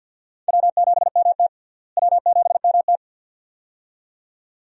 Morse code, heard as a keyed tone, about 52 wpm, 700 Hz. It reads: W7MT W7MT
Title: USA Call Signs - 4 Characters - Repeated 2x - 50wpm